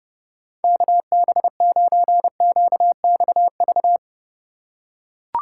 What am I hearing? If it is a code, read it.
Morse code, heard as a keyed tone, about 30 wpm, 700 Hz. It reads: KB9QXV